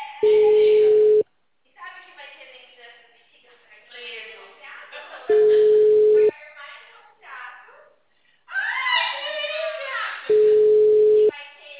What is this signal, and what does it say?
Ringback tone (Europe, 425 Hz), with speech.
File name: out- -2007- - - .3032